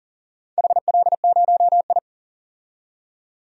Morse code, heard as a keyed tone, about 40 wpm, 700 Hz. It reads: HL0I
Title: DXCC-100 Call Signs - 4 Characters - 40wpm